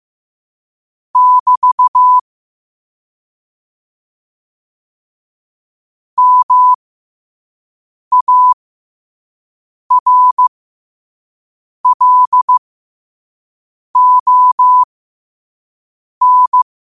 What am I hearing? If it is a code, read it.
Morse code, heard as a keyed tone, about 15 wpm, spaced as for 5 wpm, 1000 Hz. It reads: <BT> MARLON
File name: qotdfiles / 5wpm